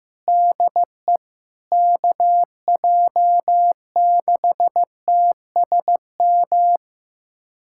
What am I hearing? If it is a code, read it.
Morse code, heard as a keyed tone, about 15 wpm, 700 Hz. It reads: DE KJ6TSM
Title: USA Callsign DE Callsign - 6 Characters - 15wpm